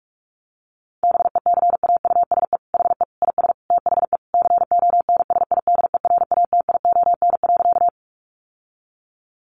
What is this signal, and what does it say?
Morse code, heard as a keyed tone, about 44 wpm, 700 Hz. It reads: BECAUSE HE IS THE CONSIDERATION.
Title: Sentences from Top 1500 Words 45wpm